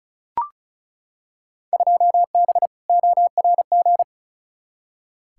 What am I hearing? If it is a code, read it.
Morse code, heard as a keyed tone, about 34 wpm, 700 Hz. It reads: 2B ORG
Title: Field Day Exchanges Encore - 35wpm